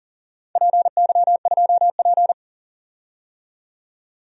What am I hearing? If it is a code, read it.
Morse code, heard as a keyed tone, about 39 wpm, 700 Hz. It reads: PY2P
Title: DXCC-100 Call Signs - 4 Characters - 40wpm